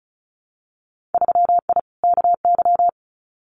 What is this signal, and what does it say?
Morse code, heard as a keyed tone, about 36 wpm, 700 Hz. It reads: 3I KY
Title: Winter Field Day Exchanges 35wpm